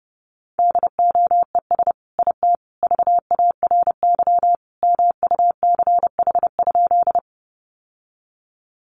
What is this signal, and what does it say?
Morse code, heard as a keyed tone, about 30 wpm, 700 Hz. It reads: DOES IT VARY MUCH?
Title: Sentences from Top 1200 Words 30wpm